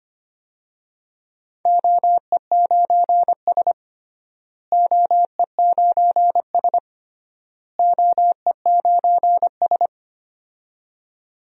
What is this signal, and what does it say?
Morse code, heard as a keyed tone, about 25 wpm, 700 Hz. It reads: OE9S OE9S OE9S
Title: DXCC-100 Call Signs - 4 Characters - Repeated 3x - 25wpm